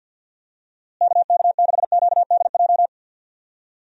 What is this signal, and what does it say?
Morse code, heard as a keyed tone, about 47 wpm, 700 Hz. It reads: KK6QDJ
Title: USA Call Signs - 6 Characters - Repeated 3x - 50wpm